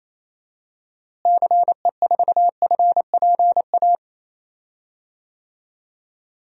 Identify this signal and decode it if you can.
Morse code, heard as a keyed tone, about 28 wpm, 700 Hz. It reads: CE4FPA
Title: DXCC-100 Call Signs - 6 Characters - Repeated 2x - 28wpm